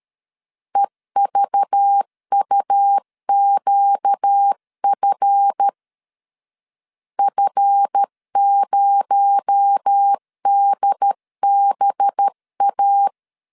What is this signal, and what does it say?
Morse code, heard as a keyed tone, about 13 wpm, 800 Hz. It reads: EVUQF F0DBA
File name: LEZIONI / 7 GF9Z / 7 GF 12wpm 2122